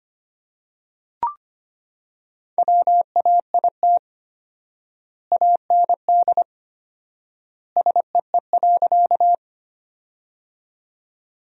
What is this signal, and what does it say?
Morse code, heard as a keyed tone, about 25 wpm, 700 Hz. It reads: WAIT AND SEE.
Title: 4x Word-Spacing - Sentences from Top 500 Words - 25wpm